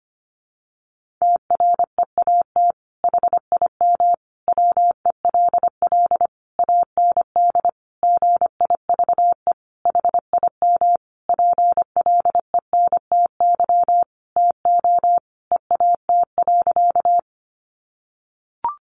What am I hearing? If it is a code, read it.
Morse code, heard as a keyed tone, about 25 wpm, 700 Hz. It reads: TREAT HIM WELL AND GIVE HIM PLENTY TO EAT.
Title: Sentences from Top 1300 Words 25wpm